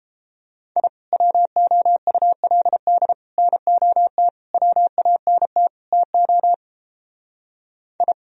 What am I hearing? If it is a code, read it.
Morse code, heard as a keyed tone, about 30 wpm, 700 Hz. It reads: I WOULD NOT WANT TO I